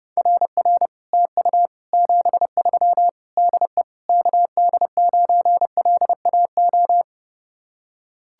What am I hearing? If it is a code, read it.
Morse code, heard as a keyed tone, about 30 wpm, 700 Hz. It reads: RR TU 73 DE KD9LAO